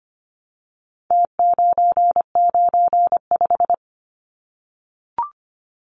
Morse code, heard as a keyed tone, about 25 wpm, 700 Hz. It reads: T995
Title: Numbers - Contest Style - 4-Digit 25wpm